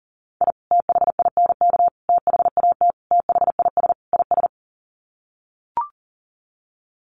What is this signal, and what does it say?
Morse code, heard as a keyed tone, about 40 wpm, 700 Hz. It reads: I THINK THAT THIS IS
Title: Sets of 5 Words Encore2 40wpm